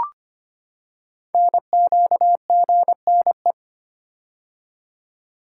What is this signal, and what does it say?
Morse code, heard as a keyed tone, about 25 wpm, 700 Hz. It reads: NQGNE
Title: Code Groups - 5 Letters - 25wpm